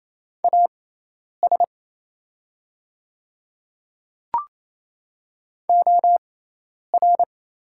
Morse code, heard as a keyed tone, about 28 wpm, spaced as for 10 wpm, 700 Hz. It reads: AS OR
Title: 6x Character Spacing - Top 100 Words - 28wpm